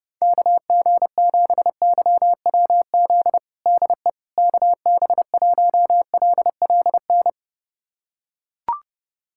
Morse code, heard as a keyed tone, about 30 wpm, 700 Hz. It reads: KG7YWZ DE KB1LLN
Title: USA Callsign DE Callsign - 6 Characters - 30wpm